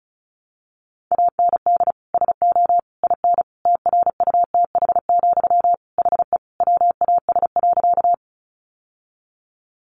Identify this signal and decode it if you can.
Morse code, heard as a keyed tone, about 34 wpm, 700 Hz. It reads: AND SO IN TRUTH, HE WAS.
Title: Sentences from Top 800 Words 35wpm